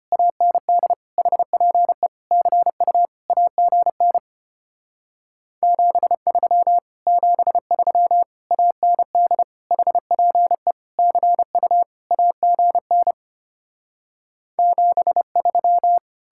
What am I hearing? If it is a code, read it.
Morse code, heard as a keyed tone, about 33 wpm, 700 Hz. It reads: AND HPE CU AGN 73 73 AND HPE CU AGN 73